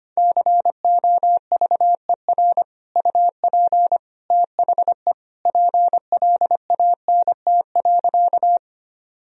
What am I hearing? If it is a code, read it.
Morse code, heard as a keyed tone, about 25 wpm, 700 Hz. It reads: COVER UP THE PLANT.
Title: Sentences from Top 1000 Words - Review Part 1 - 25wpm